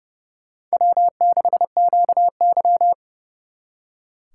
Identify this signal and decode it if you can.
Morse code, heard as a keyed tone, about 35 wpm, 700 Hz. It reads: W6QY
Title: Speed Racing - USA 1x2 Call Signs 30wpm